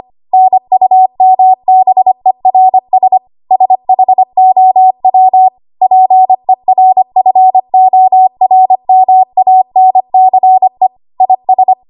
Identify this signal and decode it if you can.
Morse code, heard as a keyed tone, about 25 wpm, 750 Hz. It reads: NUMBERS SHOW PERFORMANCE IS
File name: Archive / 25 WPM / 25WPM